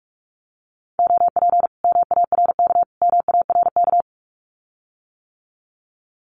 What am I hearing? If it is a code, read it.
Morse code, heard as a keyed tone, about 46 wpm, 700 Hz. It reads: OP MARK MARK
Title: Common QSO Phrases Encore 45wpm